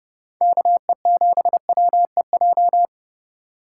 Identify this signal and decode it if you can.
Morse code, heard as a keyed tone, about 30 wpm, 700 Hz. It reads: KE7WEJ
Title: USA Call Signs - 6 Characters - 4x Character Spacing - 30wpm